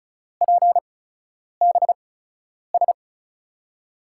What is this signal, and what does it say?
Morse code, heard as a keyed tone, about 36 wpm, spaced as for 10 wpm, 700 Hz. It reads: PBS
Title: USA Call Signs - 5 Characters - 8x Character Spacing - 35wpm